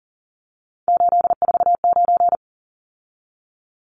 Morse code, heard as a keyed tone, about 40 wpm, 700 Hz. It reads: 849
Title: Numbers - 3-Digit 40wpm